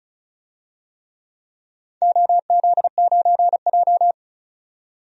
Morse code, heard as a keyed tone, about 36 wpm, 700 Hz. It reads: OZ9J
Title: DXCC-100 Call Signs - 4 Characters - 35wpm